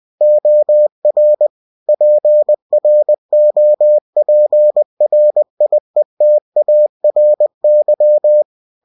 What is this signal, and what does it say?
Morse code, heard as a keyed tone, about 20 wpm, 600 Hz. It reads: OR PROPRIETARY
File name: cw-books / winnie-the-pooh / winnie-20wpm-page-0116